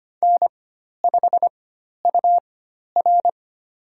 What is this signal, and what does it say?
Morse code, heard as a keyed tone, about 25 wpm, spaced as for 12 wpm, 700 Hz. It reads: N5UR
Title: USA Call Signs - 4 Characters - 4x Character Spacing - 25wpm